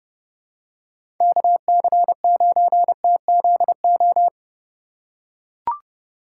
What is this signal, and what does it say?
Morse code, heard as a keyed tone, about 31 wpm, 700 Hz. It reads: KC9TZO
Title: K1USN SST QSO Practice - 30wpm